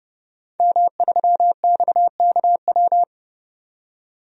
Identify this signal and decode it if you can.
Morse code, heard as a keyed tone, about 32 wpm, 700 Hz. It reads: M3XKW
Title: DXCC-100 Call Signs - 5 Characters - 30wpm